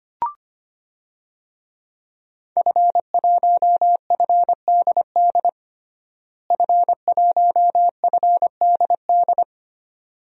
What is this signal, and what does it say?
Morse code, heard as a keyed tone, about 25 wpm, 700 Hz. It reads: F1FDD F1FDD
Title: DXCC-100 Call Signs - 5 Characters - Repeated 2x - 25wpm